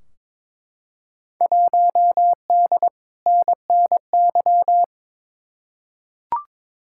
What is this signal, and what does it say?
Morse code, heard as a keyed tone, about 22 wpm, 700 Hz. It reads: 1D NNY
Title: Field Day Exchanges - 22wpm